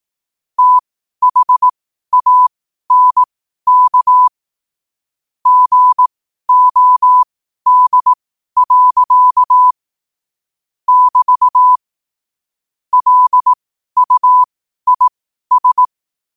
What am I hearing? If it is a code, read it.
Morse code, heard as a keyed tone, about 18 wpm, spaced as for 12 wpm, 1000 Hz. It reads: THANK GOD. <BT> LUIS
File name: qotdfiles / 12wpm